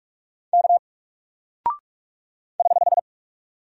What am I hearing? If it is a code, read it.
Morse code, heard as a keyed tone, about 44 wpm, 700 Hz. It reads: K <HH>
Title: Single Character (Letter:Number:Punctuation:Prosign) 45wpm